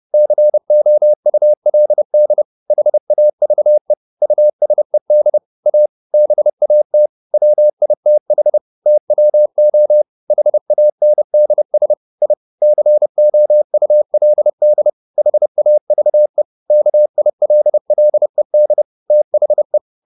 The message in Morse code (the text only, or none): COULD HAVE USED A BAT WITH TWO HANDS I COULD HAVE KILLED THE